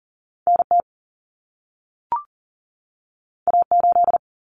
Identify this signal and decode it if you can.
Morse code, heard as a keyed tone, about 40 wpm, 700 Hz. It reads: NT A8
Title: Common Cut Numbers - 2-Digit 40wpm